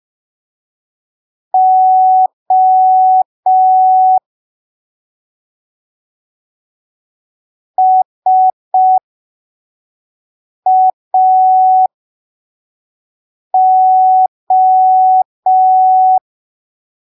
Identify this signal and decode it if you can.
Morse code, heard as a keyed tone, about 5 wpm, spaced as for 3 wpm, 750 Hz. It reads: O SAO